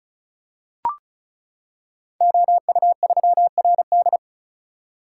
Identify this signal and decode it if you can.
Morse code, heard as a keyed tone, about 36 wpm, 690 Hz. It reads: OU3RD
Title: Code Groups - 5 Letter:Numbers - 35wpm